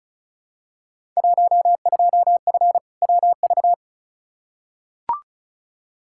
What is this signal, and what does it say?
Morse code, heard as a keyed tone, about 36 wpm, 700 Hz. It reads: 12F WV
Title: Field Day Exchanges Encore - 35wpm